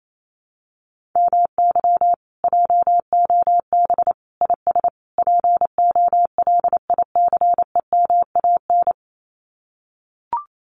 Morse code, heard as a keyed tone, about 28 wpm, 700 Hz. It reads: MY JOB IS POLICEMAN
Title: Common QSO Phrases Encore 28wpm